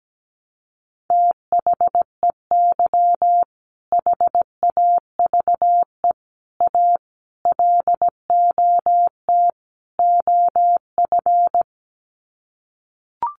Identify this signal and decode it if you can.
Morse code, heard as a keyed tone, about 17 wpm, 700 Hz. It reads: THEY HAVE A LOT OF
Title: Sets of 5 Words Encore2 17wpm